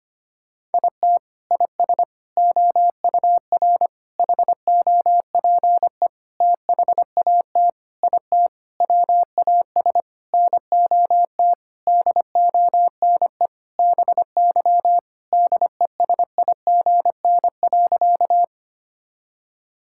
Morse code, heard as a keyed tone, about 25 wpm, 700 Hz. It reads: IT IS OUR HOPE THAT IT WAS NOT DONE BY DESIGN.
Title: Sentences from Top 1000 Words - Review Part 5 - 25wpm